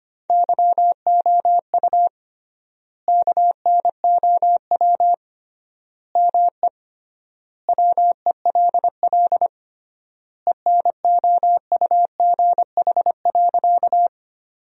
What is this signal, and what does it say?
Morse code, heard as a keyed tone, about 25 wpm, 700 Hz. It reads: YOU KNOW ME WELL ENOUGH.
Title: 3x Word-Spacing - Sentences from Top 400 Words - 25wpm